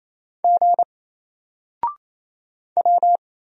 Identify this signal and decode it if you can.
Morse code, heard as a keyed tone, about 28 wpm, 700 Hz. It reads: G W
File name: Single-Letter-Number-28wpm